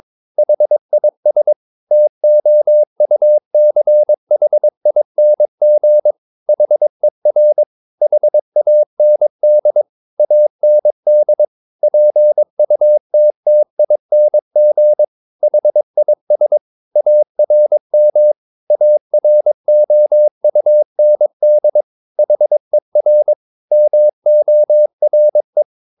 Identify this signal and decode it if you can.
Morse code, heard as a keyed tone, about 22 wpm, 600 Hz. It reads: HIS TOUCHING HER HAND AND PUTTING HIS ARM AROUND HER MORE
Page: Romanen Never af Ken Follet på engelsk.